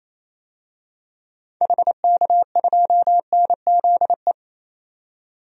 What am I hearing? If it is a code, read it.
Morse code, heard as a keyed tone, about 28 wpm, 700 Hz. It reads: HK2NZE
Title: DXCC-100 Call Signs - 6 Characters - 28wpm